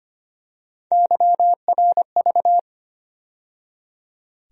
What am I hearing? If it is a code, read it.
Morse code, heard as a keyed tone, about 25 wpm, 700 Hz. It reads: YRV
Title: USA Call Sign Postfixes - 3 Letters 25wpm